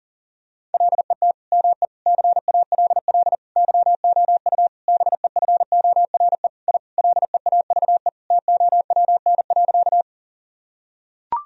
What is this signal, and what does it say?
Morse code, heard as a keyed tone, about 40 wpm, 700 Hz. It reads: LET ME CALL YOU BEFORE I LEAVE TOWN.
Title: Sentences from Top 1000 Words - Review Part 2 - 40wpm